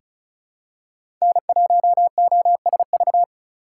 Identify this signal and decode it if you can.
Morse code, heard as a keyed tone, about 34 wpm, 700 Hz. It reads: N1OSV
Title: USA Call Signs - 5 Characters - 3x Character Spacing - 35wpm